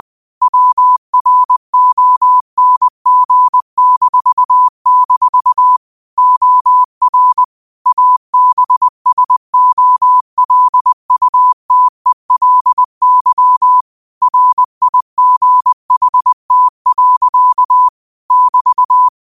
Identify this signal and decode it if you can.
Morse code, heard as a keyed tone, about 20 wpm, 1000 Hz. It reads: WRONG-- OR ABSOLUTELY RIGHT. <BT>